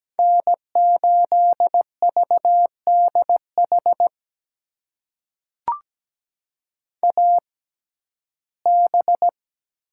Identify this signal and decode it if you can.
Morse code, heard as a keyed tone, about 17 wpm, spaced as for 12 wpm, 700 Hz. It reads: N8VDH AB
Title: USA Call Signs - 5 Characters - 6x Character Spacing - 17wpm